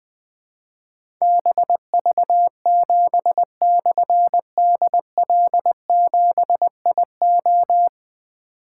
Morse code, heard as a keyed tone, about 20 wpm, 700 Hz. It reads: BV7/DL7IO
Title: Difficult Call Signs 20wpm